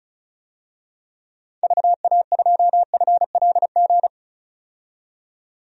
Morse code, heard as a keyed tone, about 36 wpm, 700 Hz. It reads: VA2FLG
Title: DXCC-100 Call Signs - 6 Characters - 35wpm